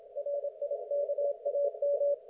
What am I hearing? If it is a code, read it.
Morse code, heard as a keyed tone, about 27 wpm, 560 Hz. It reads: RIKAM
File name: Archiv / CUC179 / ok1fhi cul179